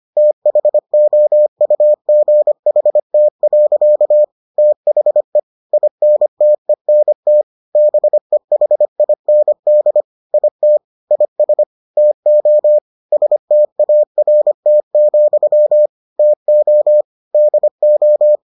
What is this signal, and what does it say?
Morse code, heard as a keyed tone, about 25 wpm, 600 Hz. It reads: THOUGHT. THE INTENT BEHIND IT IS TO START, TO DO